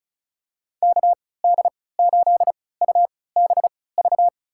Morse code, heard as a keyed tone, about 34 wpm, spaced as for 20 wpm, 700 Hz. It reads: KD8UBV
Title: USA Call Signs - 6 Characters - 3x Character Spacing - 35wpm